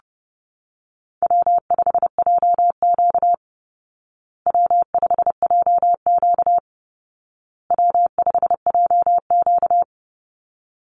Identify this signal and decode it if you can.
Morse code, heard as a keyed tone, about 30 wpm, 700 Hz. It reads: W5JQ W5JQ W5JQ
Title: Call Signs Repeated 2x - 30wpm